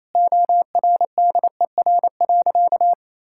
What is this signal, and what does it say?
Morse code, heard as a keyed tone, about 28 wpm, 700 Hz. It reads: ORDER.
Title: Sentences from Top 1600 Words 28wpm